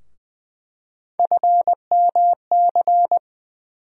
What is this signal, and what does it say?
Morse code, heard as a keyed tone, about 20 wpm, 700 Hz. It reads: FMC
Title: USA Call Sign Postfixes - 3 Letters 20wpm